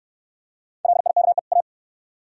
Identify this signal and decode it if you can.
Morse code, heard as a keyed tone, about 62 wpm, 700 Hz. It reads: LIKE A